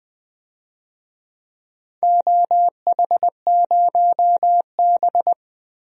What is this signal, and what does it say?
Morse code, heard as a keyed tone, about 20 wpm, 700 Hz. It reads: OH0B
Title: DXCC-100 Call Signs - 4 Characters - Repeated 3x - 20wpm